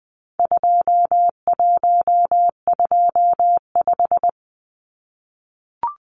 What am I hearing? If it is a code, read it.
Morse code, heard as a keyed tone, about 20 wpm, 700 Hz. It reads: 2125
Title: Numbers - 4-Digit 20wpm